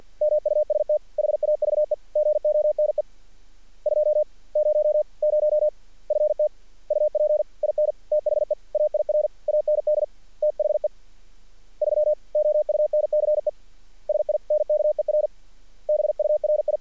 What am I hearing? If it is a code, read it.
Morse code, heard as a keyed tone, about 48 wpm, 590 Hz. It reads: MUST HAVE GONE 200 FT UP IN THE AIR AND THE 3 OUNCE SINKER BURI
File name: archived / CRSnet